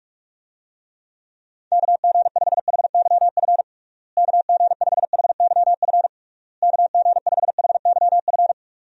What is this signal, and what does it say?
Morse code, heard as a keyed tone, about 46 wpm, 700 Hz. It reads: KG5HYF KG5HYF KG5HYF
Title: USA Call Signs - 6 Characters - Repeated 3x - 45wpm